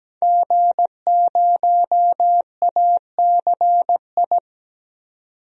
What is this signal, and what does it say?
Morse code, heard as a keyed tone, about 17 wpm, 700 Hz. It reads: G0ACI
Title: DXCC-100 Call Signs - 5 Characters - 17wpm